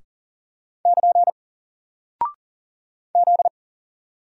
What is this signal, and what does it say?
Morse code, heard as a keyed tone, about 38 wpm, 700 Hz. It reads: <KN> Z
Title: Single Character (Letter:Number:Punctuation:Prosign) Alt 40wpm